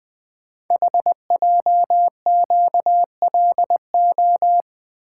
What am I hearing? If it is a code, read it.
Morse code, heard as a keyed tone, about 21 wpm, 700 Hz. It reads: HJQLO
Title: Code Groups - 5 Letters - 20wpm